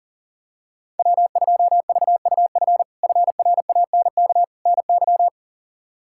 Morse code, heard as a keyed tone, about 40 wpm, 700 Hz. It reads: W2VUF FRANK NY
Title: K1USN SST QSO Practice - 40wpm